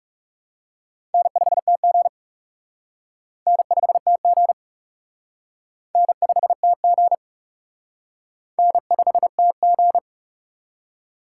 Morse code, heard as a keyed tone, about 36 wpm, 700 Hz. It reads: N5TG N5TG N5TG N5TG
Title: Speed Racing - USA 1x2 Call Signs 30wpm